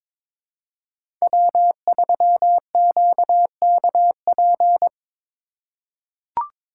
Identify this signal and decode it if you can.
Morse code, heard as a keyed tone, about 22 wpm, 700 Hz. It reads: W3QKP
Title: USA Call Signs - 3x Character Spacing - 22wpm